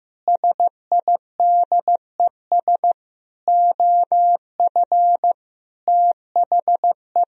Morse code, heard as a keyed tone, about 15 wpm, 700 Hz. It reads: SIDES OF THE
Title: Speed-Racing - Sets of 5 Words 15wpm